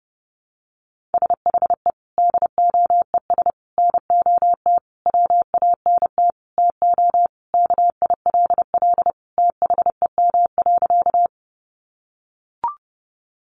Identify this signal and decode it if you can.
Morse code, heard as a keyed tone, about 28 wpm, 700 Hz. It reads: SHE DOES NOT WANT TO KILL THEM.
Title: Easy Phrases 30wpm